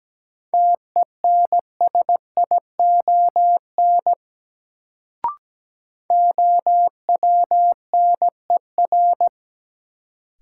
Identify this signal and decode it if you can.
Morse code, heard as a keyed tone, about 17 wpm, 700 Hz. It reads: TENSION OWNER
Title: Sentences from Top 1600 Words - New Word Drill Alt - 17wpm